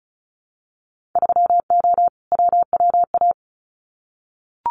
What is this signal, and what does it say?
Morse code, heard as a keyed tone, about 36 wpm, 700 Hz. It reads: 3O WWA
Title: Winter Field Day Exchanges Encore 35wpm